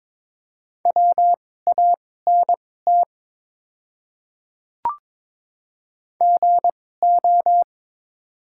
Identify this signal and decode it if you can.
Morse code, heard as a keyed tone, about 22 wpm, spaced as for 16 wpm, 700 Hz. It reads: WANT GO
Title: 2x Character Spacing - Top 200 Words - 22wpm